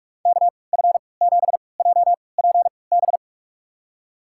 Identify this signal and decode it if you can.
Morse code, heard as a keyed tone, about 46 wpm, spaced as for 26 wpm, 700 Hz. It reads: KF7JPB
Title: USA Call Signs - 6 Characters - 3x Character Spacing - 45wpm